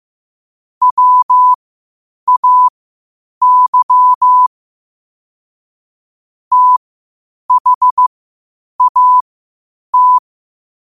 Morse code, heard as a keyed tone, about 15 wpm, spaced as for 8 wpm, 1000 Hz. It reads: WAY THAT
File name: qotdfiles / 8wpm